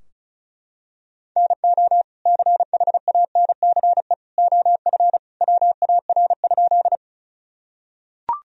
Morse code, heard as a keyed tone, about 34 wpm, 700 Hz. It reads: NO CHANCE OF WAR?
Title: Sentences from Top 1000 Words - Review Part 2 - 35wpm